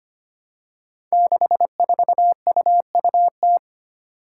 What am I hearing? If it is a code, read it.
Morse code, heard as a keyed tone, about 25 wpm, 700 Hz. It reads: 64UUT
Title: Five Challenging Letter-Number Combinations 25wpm